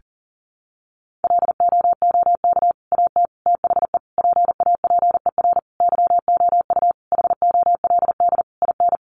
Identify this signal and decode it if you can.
Morse code, heard as a keyed tone, about 40 wpm, 700 Hz. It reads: LOOK AT THE PAPER YOU HOLD IN
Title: Sentences from Top 400 Words 40wpm